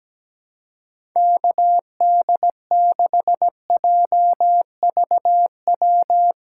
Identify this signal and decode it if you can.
Morse code, heard as a keyed tone, about 17 wpm, 700 Hz. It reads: KD6JVW
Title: USA Call Signs - 6 Characters - 4x Character Spacing - 17wpm